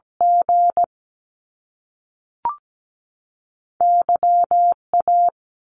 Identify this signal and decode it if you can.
Morse code, heard as a keyed tone, about 17 wpm, 700 Hz. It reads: G Y A
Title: Top Two Letter Combinations in Words 17wpm